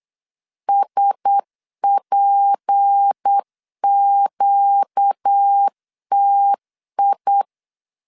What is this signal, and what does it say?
Morse code, heard as a keyed tone, about 8 wpm, 800 Hz. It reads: SPQTI
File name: LEZIONI / 6 PKX0 / 6 P 8wpm 3025